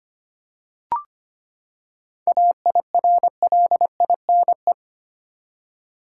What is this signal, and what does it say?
Morse code, heard as a keyed tone, about 25 wpm, 700 Hz. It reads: AIRLINE
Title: Seven Letter Words - 25wpm